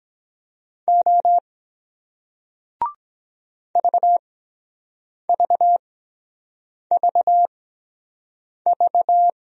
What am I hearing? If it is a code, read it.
Morse code, heard as a keyed tone, about 22 wpm, spaced as for 7 wpm, 700 Hz. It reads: O VVVV